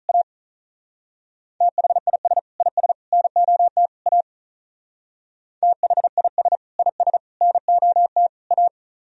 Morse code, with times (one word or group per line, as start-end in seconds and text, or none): 0.1-0.2 A
1.6-2.4 THIS
2.6-2.9 IS
3.1-3.9 NOT
4.1-4.2 A
5.6-6.6 THIS
6.8-7.2 IS
7.4-8.3 NOT
8.5-8.7 A